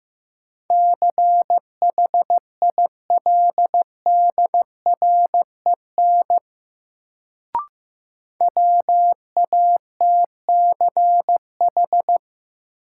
Morse code, heard as a keyed tone, about 15 wpm, 700 Hz. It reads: CHILDREN WATCH
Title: Sentences from Top 300 Words - New Word Drill - 15wpm